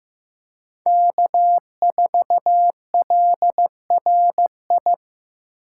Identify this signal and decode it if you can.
Morse code, heard as a keyed tone, about 18 wpm, 700 Hz. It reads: K4LRI
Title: Speed Racing - USA Call Signs 15wpm